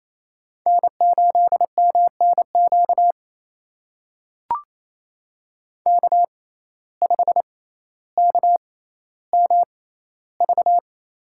Morse code, heard as a keyed tone, about 28 wpm, spaced as for 14 wpm, 700 Hz. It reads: N8MNQ K5KMV
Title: USA Call Signs - 5 Characters - 6x Character Spacing - 28wpm